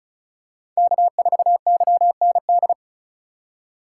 Morse code, heard as a keyed tone, about 36 wpm, 700 Hz. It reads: K4YND
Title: K1USN SST QSO Practice - 35wpm